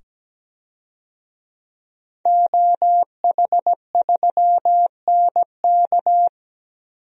Morse code, heard as a keyed tone, about 17 wpm, 700 Hz. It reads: OH3NK
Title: DXCC-100 Call Signs - 5 Characters - 17wpm